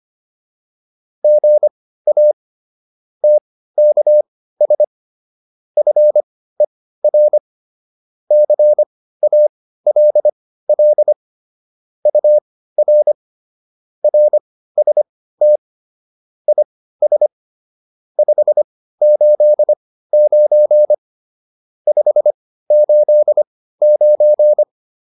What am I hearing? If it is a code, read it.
Morse code, heard as a keyed tone, about 25 wpm, spaced as for 15 wpm, 600 Hz. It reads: GA TKS FER CALL UR RST IS 589 589
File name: QSO404 15